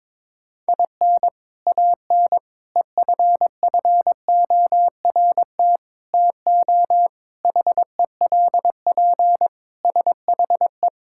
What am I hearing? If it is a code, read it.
Morse code, heard as a keyed tone, about 22 wpm, 700 Hz. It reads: IN AN EFFORT TO HELP SHE
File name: practice / 22wpm / Sentences with 4-Word Phrases - 22wpm